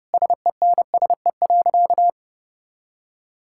Morse code, heard as a keyed tone, about 30 wpm, 700 Hz. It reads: SENSE.
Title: Sentences from Top 1000 Words - Review Part 5 - 30wpm